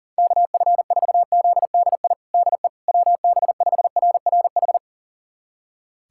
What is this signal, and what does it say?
Morse code, heard as a keyed tone, about 37 wpm, 700 Hz. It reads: KF4ZDI DE WB5RRH
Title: USA Callsign DE Callsign - 6 Characters - 40wpm